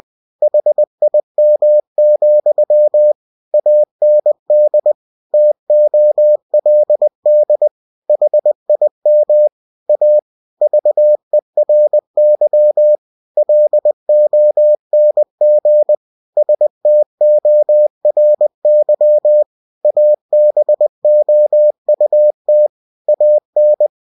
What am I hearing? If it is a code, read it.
Morse code, heard as a keyed tone, about 20 wpm, 600 Hz. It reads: HIM, AND TOLD HIM A VERY LONG STORY ABOUT AN
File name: cw-books / winnie-the-pooh / winnie-20wpm-page-0101